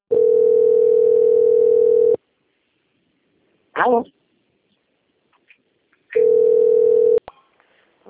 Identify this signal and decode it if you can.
Ringback tone (North America), 440 + 480 Hz, with speech.
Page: Limeira - SP